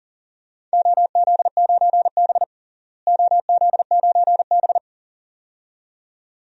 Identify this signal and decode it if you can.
Morse code, heard as a keyed tone, about 37 wpm, 700 Hz. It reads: OZ9B OZ9B
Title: DXCC-100 Call Signs - 4 Characters - Repeated 2x - 40wpm